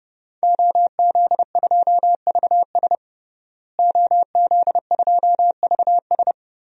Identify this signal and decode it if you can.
Morse code, heard as a keyed tone, about 30 wpm, 700 Hz. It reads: OZ2VS OZ2VS
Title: DXCC-100 Call Signs - 5 Characters - Repeated 2x - 30wpm